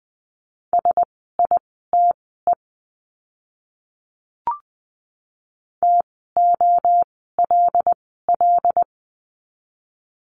Morse code, heard as a keyed tone, about 20 wpm, spaced as for 14 wpm, 700 Hz. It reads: SITE TOLL
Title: 075 Lesson 13f - Letter L - Words 20wpm